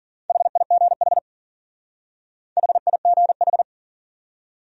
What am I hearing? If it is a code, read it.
Morse code, heard as a keyed tone, about 50 wpm, 700 Hz. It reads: HIGH HIGH
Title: Speed-Racing Top 200 Words 40wpm